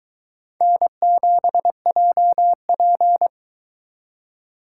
Morse code, heard as a keyed tone, about 22 wpm, 700 Hz. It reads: N7JP